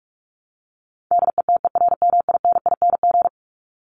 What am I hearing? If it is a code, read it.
Morse code, heard as a keyed tone, about 44 wpm, 700 Hz. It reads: DETERMINING
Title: ING Suffix - Top 100 Words with Base Words 45wpm